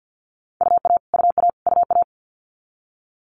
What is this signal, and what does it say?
Morse code, heard as a keyed tone, about 50 wpm, 690 Hz. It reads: UA UA UA
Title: Multi-letter Phonemes Repeated 3x 50wpm